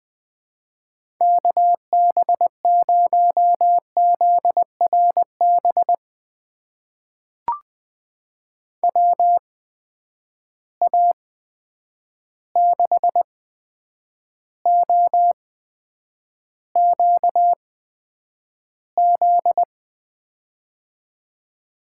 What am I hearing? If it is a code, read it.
Morse code, heard as a keyed tone, about 21 wpm, spaced as for 9 wpm, 700 Hz. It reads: KB0ZRB WA6OQZ
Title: USA Call Signs - 6 Characters - 8x Character Spacing - 20wpm